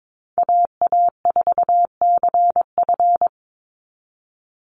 Morse code, heard as a keyed tone, about 22 wpm, 700 Hz. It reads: AA4CF
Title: Call Signs Repeated 2x - 22wpm